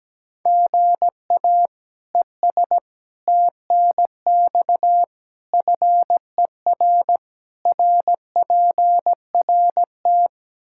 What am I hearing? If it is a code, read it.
Morse code, heard as a keyed tone, about 17 wpm, 700 Hz. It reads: GA ES TNX FER RPRT
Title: Practice QSO - Exchange 1 - Beginning - 17wpm